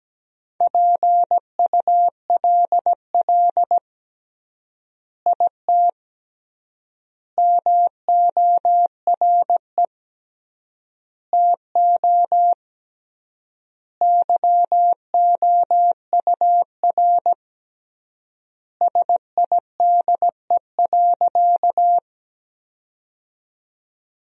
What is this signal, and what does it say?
Morse code, heard as a keyed tone, about 17 wpm, 700 Hz. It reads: PULL IT MORE TO YOUR SIDE.
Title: 3x Word-Spacing - Sentences from Top 400 Words - 17wpm